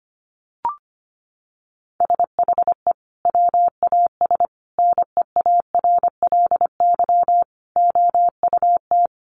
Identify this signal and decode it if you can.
Morse code, heard as a keyed tone, about 25 wpm, 700 Hz. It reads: SHE WAS NEARLY OUT
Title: Sentences from Top 1300 Words 25wpm